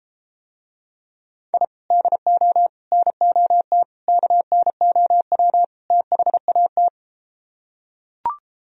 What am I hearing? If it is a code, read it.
Morse code, heard as a keyed tone, about 33 wpm, 700 Hz. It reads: I DO NOT KNOW THAT